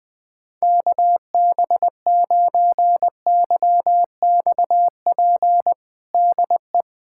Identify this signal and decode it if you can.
Morse code, heard as a keyed tone, about 21 wpm, 700 Hz. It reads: KB9YXP DE
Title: USA Callsign DE Callsign 20wpm